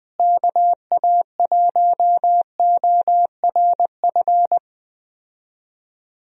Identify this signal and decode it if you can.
Morse code, heard as a keyed tone, about 20 wpm, 700 Hz. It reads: KA1ORF
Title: USA Call Signs - 6 Characters - Repeated 3x - 20wpm